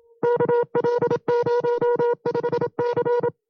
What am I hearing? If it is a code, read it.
Morse code, heard as a keyed tone, about 27 wpm, 480 Hz. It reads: KL05C
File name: FreeFiles / CW-practice / 25WPM-side2